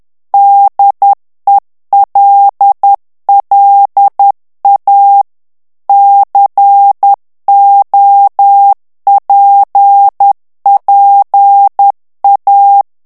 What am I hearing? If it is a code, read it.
Morse code, heard as a keyed tone, about 10 wpm, 800 Hz. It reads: DELLA COPPA
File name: ITALIEN texte clair 2 vitesse 10 mots minute